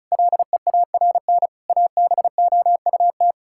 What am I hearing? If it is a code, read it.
Morse code, heard as a keyed tone, about 36 wpm, 700 Hz. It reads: LEARN ABOUT
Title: Sentences from Top 1000 Words - Review Part 6 - 35wpm